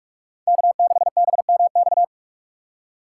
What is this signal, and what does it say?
Morse code, heard as a keyed tone, about 44 wpm, 690 Hz. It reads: K6BMX
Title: USA Call Signs - 5 Characters - 6x Character Spacing - 45wpm